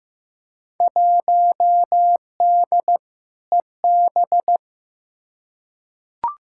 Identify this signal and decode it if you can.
Morse code, heard as a keyed tone, about 15 wpm, 700 Hz. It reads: 1D EB
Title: Field Day Exchanges - 15wpm